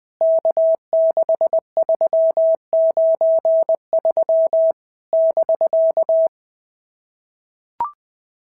Morse code, heard as a keyed tone, about 20 wpm, 650 Hz. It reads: K6393 <BK>
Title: POTA QSO Practice - 20wpm